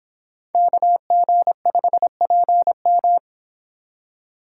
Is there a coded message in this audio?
Yes: Morse code from 0.5 to 3.2 s, heard as a keyed tone, about 25 wpm, 700 Hz. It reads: KG5PM